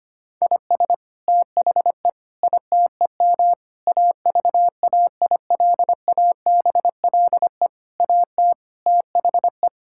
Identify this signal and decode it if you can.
Morse code, heard as a keyed tone, about 25 wpm, 700 Hz. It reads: IS THE ITEM AVAILABLE AT THE
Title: Sentences from Top 1000 Words - Review Part 6 - 25wpm